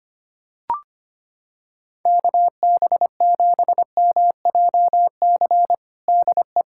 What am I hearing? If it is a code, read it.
Morse code, heard as a keyed tone, about 25 wpm, 700 Hz. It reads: KB7MJC DE
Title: USA Callsign DE Callsign 25wpm